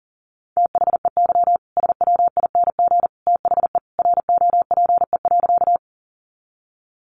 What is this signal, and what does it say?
Morse code, heard as a keyed tone, about 40 wpm, 700 Hz. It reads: THEY SWING THE ROPE.
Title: Easy Phrases 40wpm